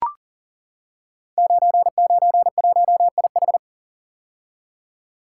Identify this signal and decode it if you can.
Morse code, heard as a keyed tone, about 38 wpm, 700 Hz. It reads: 991IH
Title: Code Groups - 5 Letter:Numbers - 40wpm